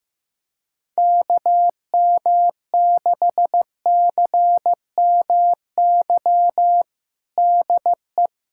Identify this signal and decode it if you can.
Morse code, heard as a keyed tone, about 16 wpm, 700 Hz. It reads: KM6CMY DE
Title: USA Callsign DE Callsign - 6 Characters - 15wpm